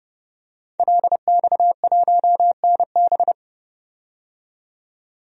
Morse code, heard as a keyed tone, about 32 wpm, 700 Hz. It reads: LX1NB
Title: DXCC-100 Call Signs - 5 Characters - 30wpm